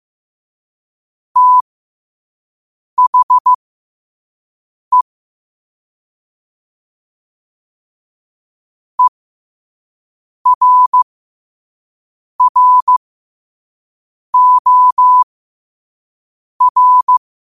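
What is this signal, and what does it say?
Morse code, heard as a keyed tone, about 15 wpm, spaced as for 5 wpm, 1000 Hz. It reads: THE ERROR
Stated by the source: Morse practice at 5 wpm.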